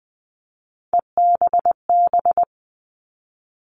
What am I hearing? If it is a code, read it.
Morse code, heard as a keyed tone, about 20 wpm, 700 Hz. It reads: EBB
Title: All Cut Numbers - 3-Digit 20wpm